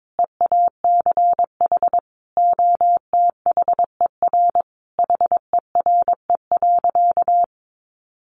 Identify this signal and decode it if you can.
Morse code, heard as a keyed tone, about 22 wpm, 700 Hz. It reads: EACH OTHER HERE.
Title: Sentences from Top 1200 Words 22wpm